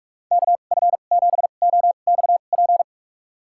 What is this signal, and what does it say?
Morse code, heard as a keyed tone, about 44 wpm, spaced as for 32 wpm, 700 Hz. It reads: KF7OXP
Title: USA Call Signs - 6 Characters - 2x Character Spacing - 45wpm